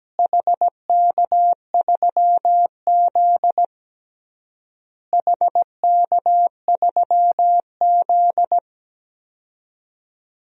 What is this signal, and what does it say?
Morse code, heard as a keyed tone, about 17 wpm, 700 Hz. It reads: HK3Z HK3Z
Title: DXCC-100 Call Signs - 4 Characters - Repeated 3x - 17wpm